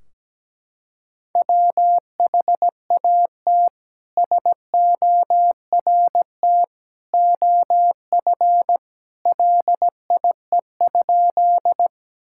Morse code, heard as a keyed tone, about 17 wpm, 700 Hz. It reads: WHAT SORT OF LIE?
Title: Sentences from Top 1600 Words 17wpm